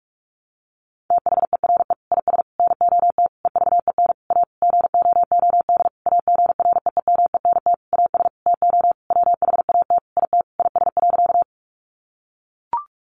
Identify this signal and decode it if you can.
Morse code, heard as a keyed tone, about 44 wpm, 700 Hz. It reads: THERE IS NOT EVEN A GOOD AGREEMENT AS TO WHAT IT IS.
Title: Sentences from Top 1500 Words 45wpm